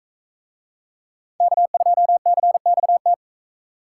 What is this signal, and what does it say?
Morse code, heard as a keyed tone, about 40 wpm, 700 Hz. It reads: K2CXT